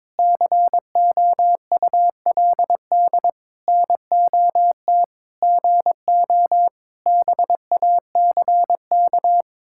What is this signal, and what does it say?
Morse code, heard as a keyed tone, about 22 wpm, 700 Hz. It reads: COULD NOT GO BACK
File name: Sentences with 7-Word Phrases - 22wpm